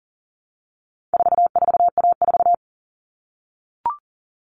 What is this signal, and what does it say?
Morse code, heard as a keyed tone, about 40 wpm, 700 Hz. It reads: 44A4
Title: All Cut Numbers - 4-Digit 40wpm